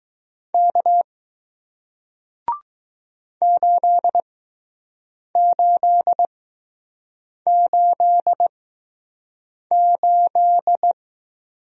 Morse code, heard as a keyed tone, about 20 wpm, spaced as for 6 wpm, 700 Hz. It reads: K 8888